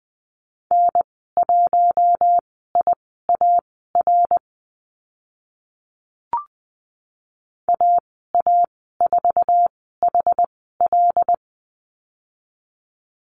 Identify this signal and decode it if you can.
Morse code, heard as a keyed tone, about 20 wpm, spaced as for 15 wpm, 700 Hz. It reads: N1IAR AA4HL
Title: 076 Lesson 13g - Letter L - Call Signs 20wpm